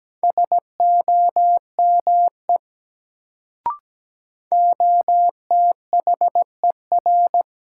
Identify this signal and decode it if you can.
Morse code, heard as a keyed tone, about 17 wpm, 700 Hz. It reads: SOME OTHER
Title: Sentences from Top 100 Words - New Word Drill - 17wpm